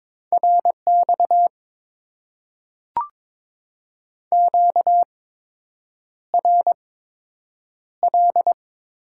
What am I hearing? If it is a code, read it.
Morse code, heard as a keyed tone, about 22 wpm, spaced as for 8 wpm, 700 Hz. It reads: RX QRL
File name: Top 90 QSO Elements - 8x Character Spacing - 22wpm